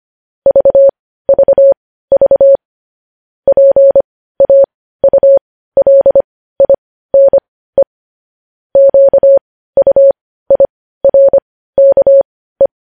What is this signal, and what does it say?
Morse code, heard as a keyed tone, about 25 wpm, spaced as for 15 wpm, 550 Hz. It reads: VVV PAULINE QUIRKE
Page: News Headlines in Morse Code at 15 WPM